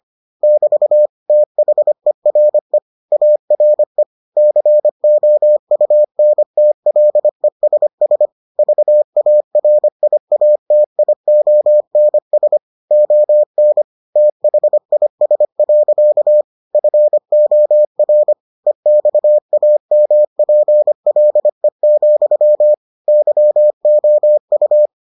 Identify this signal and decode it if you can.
Morse code, heard as a keyed tone, about 25 wpm, 600 Hz. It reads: <BT> THERE ARE COUNTLESS VARIATIONS ON THIS. FOR EXAMPLE, YOU